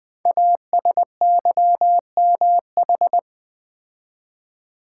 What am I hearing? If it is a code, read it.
Morse code, heard as a keyed tone, about 20 wpm, 700 Hz. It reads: ASYMH
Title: Code Groups - 5 Letters - 20wpm